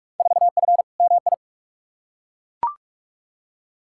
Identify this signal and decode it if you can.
Morse code, heard as a keyed tone, about 42 wpm, 700 Hz. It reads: 4F MI
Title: Field Day Exchanges Encore - 45wpm